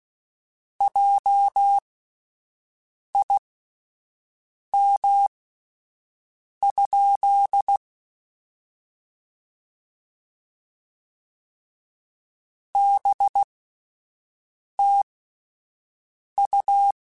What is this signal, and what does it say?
Morse code, heard as a keyed tone, about 16 wpm, spaced as for 5 wpm, 800 Hz. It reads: JIM? BTU
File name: QSOs / QSOs-mp3s / WD0BGP 1@13